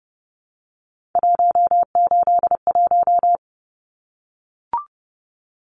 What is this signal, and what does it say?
Morse code, heard as a keyed tone, about 30 wpm, 700 Hz. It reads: 181
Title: Numbers - 3-Digit 30wpm